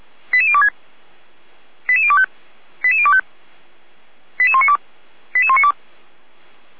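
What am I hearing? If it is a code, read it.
Five-tone selective call (ZVEI), heard as quick runs of five tones, tones of about 70 ms. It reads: 88026 88025 88025 88192 88192
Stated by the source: ZVEI five-tone call 88102 received by SDR, channel 455, Feuerwehr Orlamuende.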